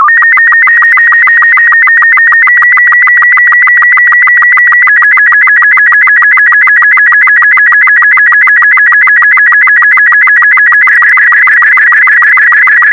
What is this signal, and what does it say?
SSTV (slow-scan television), Robot 36.